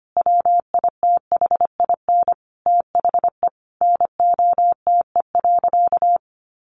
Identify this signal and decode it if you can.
Morse code, heard as a keyed tone, about 25 wpm, 700 Hz. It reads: WITHIN THE NOTE.
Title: Sentences from Top 1500 Words 25wpm